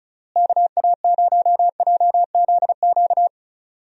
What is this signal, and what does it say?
Morse code, heard as a keyed tone, about 36 wpm, 700 Hz. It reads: KA0JZQ
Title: USA Call Signs - 6 Characters - Repeated 3x - 35wpm